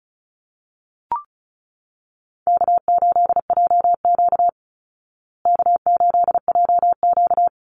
Morse code, heard as a keyed tone, about 34 wpm, 700 Hz. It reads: K8JQ K8JQ
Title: Call Signs Repeated 3x - 35wpm